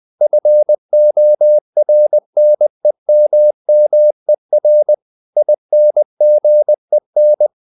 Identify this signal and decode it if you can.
Morse code, heard as a keyed tone, about 20 wpm, 600 Hz. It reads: FORNEMMER INGEN
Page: Ferie 20 wpm